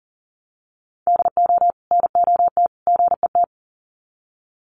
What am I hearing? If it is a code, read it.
Morse code, heard as a keyed tone, about 40 wpm, 700 Hz. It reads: DO NOT GET
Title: Sets of 3 Words 40wpm